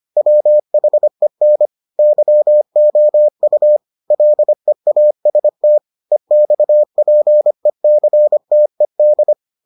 Morse code, heard as a keyed tone, about 25 wpm, 600 Hz. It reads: WHEN YOU LEAST EXPECTED